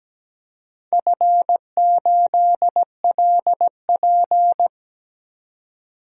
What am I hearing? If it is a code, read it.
Morse code, heard as a keyed tone, about 17 wpm, 700 Hz. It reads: F8LP
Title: DXCC-100 Call Signs - 4 Characters - 17wpm